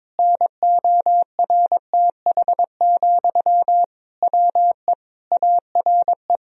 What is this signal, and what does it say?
Morse code, heard as a keyed tone, about 22 wpm, 700 Hz. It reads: NORTH, WE ARE
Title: Sentences from Top 1000 Words - Review Part 1 - 22wpm